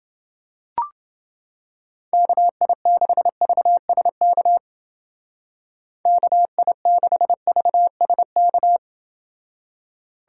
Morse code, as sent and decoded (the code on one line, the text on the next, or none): -.- .. -.... ...- ... -.- / -.- .. -.... ...- ... -.-
KI6VSK KI6VSK